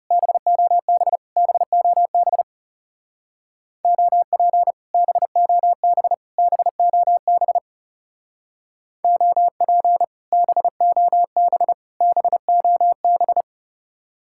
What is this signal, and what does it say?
Morse code, heard as a keyed tone, about 39 wpm, 700 Hz. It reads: BOB BOB OP BOB BOB OP BOB BOB